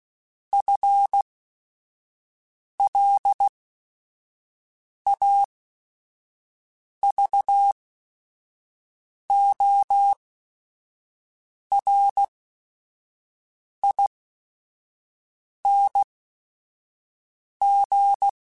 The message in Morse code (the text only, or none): FLAVORING